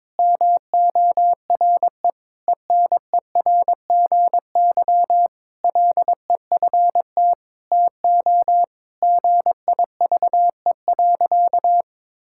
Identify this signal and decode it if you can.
Morse code, heard as a keyed tone, about 22 wpm, 700 Hz. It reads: MORE ENERGY LEFT TO GIVE.
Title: Sentences from Top 1000 Words - Review Part 6 - 22wpm